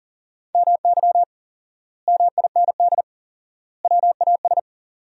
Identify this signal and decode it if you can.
Morse code, heard as a keyed tone, about 40 wpm, 700 Hz. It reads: MY MIND WAS
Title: 4x Word-Spacing - Sentences from Top 500 Words - 40wpm